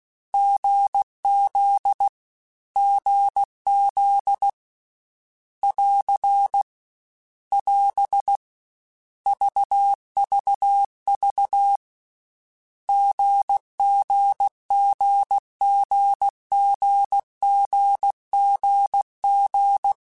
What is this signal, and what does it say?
Morse code, heard as a keyed tone, about 16 wpm, 800 Hz. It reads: GZ GZ <AR> <AS> VVV GGGGGGGG
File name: Short Runs / Short Runs-mp3s / GZ GZ7 @15